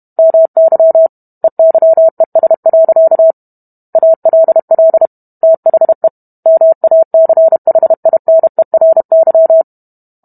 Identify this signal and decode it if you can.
Morse code, heard as a keyed tone, about 32 wpm, 650 Hz. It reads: MY EYES. ALL THE MACHINERY